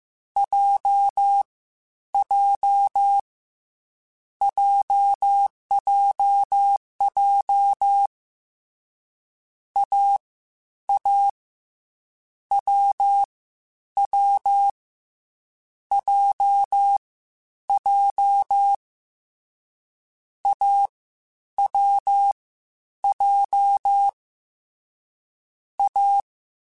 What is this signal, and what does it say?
Morse code, heard as a keyed tone, about 15 wpm, 800 Hz. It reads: J J JJJ A A W W J J A W J A